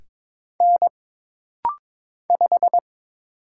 Morse code, heard as a keyed tone, about 22 wpm, 700 Hz. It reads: N 5
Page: This rapid-fire timing is geared to push your ICR (Instant Character Recognition) skills.